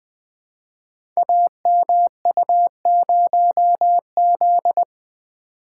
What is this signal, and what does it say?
Morse code, heard as a keyed tone, about 20 wpm, 700 Hz. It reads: AMU0Z
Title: Code Groups - 5 Letter:Numbers - 20wpm